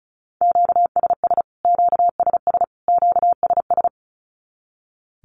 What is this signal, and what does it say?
Morse code, heard as a keyed tone, about 34 wpm, 700 Hz. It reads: QSS QSS QSS
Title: Q-Codes Repeated 3x 35wpm